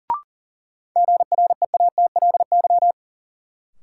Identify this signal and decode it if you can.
Morse code, heard as a keyed tone, about 40 wpm, 700 Hz. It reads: GREATLY
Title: Sentences from Top 1400 Words - New Word Drill - 40wpm